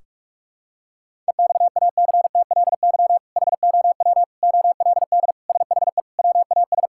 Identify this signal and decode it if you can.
Morse code, heard as a keyed tone, about 46 wpm, 700 Hz. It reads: EXACTLY HOW OLD SHE WAS
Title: Sentences from Top 1600 Words 45wpm